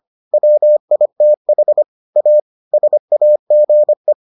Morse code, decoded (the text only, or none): WITH A SAGE